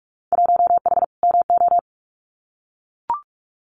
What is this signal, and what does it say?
Morse code, heard as a keyed tone, about 44 wpm, 700 Hz. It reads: 1H MO
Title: Winter Field Day Exchanges 45wpm